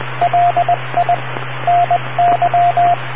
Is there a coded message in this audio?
Yes: Morse code from 0.2 to 2.9 s, heard as a keyed tone, about 20 wpm, 700 Hz, with radio noise. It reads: LI NY